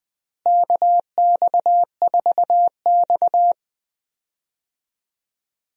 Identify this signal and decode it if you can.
Morse code, heard as a keyed tone, about 20 wpm, 700 Hz. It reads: KX4X
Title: USA Call Signs - 4 Characters - Repeated 3x - 20wpm